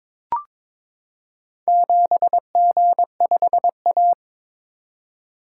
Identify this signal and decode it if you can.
Morse code, heard as a keyed tone, about 22 wpm, 700 Hz. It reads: 7G5A
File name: practice / 22wpm / Four Letter-Number Combinations - 22wpm